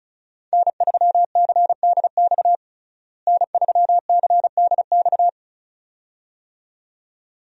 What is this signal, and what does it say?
Morse code, heard as a keyed tone, about 36 wpm, 700 Hz. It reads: N3CDX N3CDX
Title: USA Call Signs - 5 Characters - Repeated 2x - 35wpm